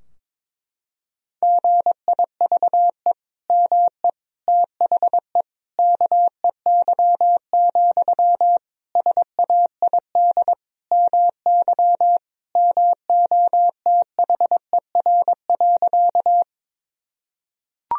Morse code, heard as a keyed tone, about 22 wpm, 700 Hz. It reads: GIVE ME THE KEY, SAID MY MOTHER.
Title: Sentences from Top 1000 Words - Review Part 4 - 22wpm